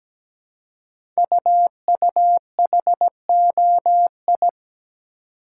Morse code, heard as a keyed tone, about 17 wpm, 700 Hz. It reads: UUHOI
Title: Code Groups - 5 Letters - 17wpm